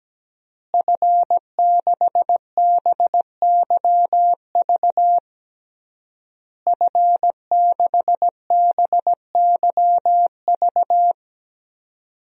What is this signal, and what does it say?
Morse code, heard as a keyed tone, about 17 wpm, 700 Hz. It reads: F6BYV F6BYV
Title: DXCC-100 Call Signs - 5 Characters - Repeated 3x - 17wpm